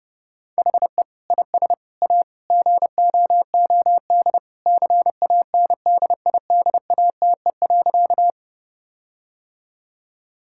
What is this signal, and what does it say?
Morse code, heard as a keyed tone, about 32 wpm, 700 Hz. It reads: HE IS A GOOD CANDIDATE.
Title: Sentences from Top 1600 Words 30wpm